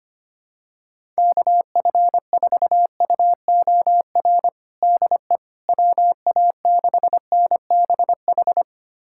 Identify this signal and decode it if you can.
Morse code, heard as a keyed tone, about 25 wpm, 700 Hz. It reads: KF4UOR DE WA6NBH
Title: USA Callsign DE Callsign - 6 Characters - 25wpm